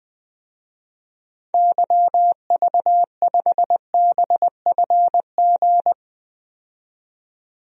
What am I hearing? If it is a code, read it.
Morse code, heard as a keyed tone, about 20 wpm, 700 Hz. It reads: YV5BFG
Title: DXCC-100 Call Signs - 6 Characters - 20wpm